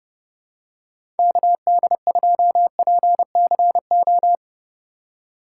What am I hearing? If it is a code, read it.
Morse code, heard as a keyed tone, about 32 wpm, 700 Hz. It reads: KD2PCO
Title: USA Call Signs - 6 Characters - 2x Character Spacing - 30wpm